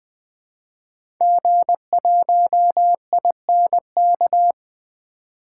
Morse code, heard as a keyed tone, about 21 wpm, 700 Hz. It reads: G1INK
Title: 2023 SOTA Activator Call Signs - 20wpm